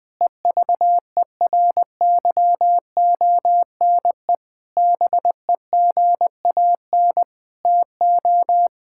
Morse code, heard as a keyed tone, about 21 wpm, 700 Hz. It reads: EVERYONE BEGAN TO
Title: Sentences from Top 1000 Words - Review Part 5 - 20wpm